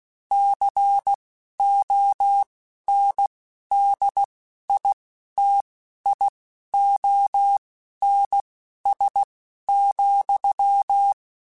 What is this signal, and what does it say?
Morse code, heard as a keyed tone, about 16 wpm, spaced as for 11 wpm, 800 Hz. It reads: CONDITIONS,
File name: Technical Subjects / Technical Subjects-mp3s / AND Gate @10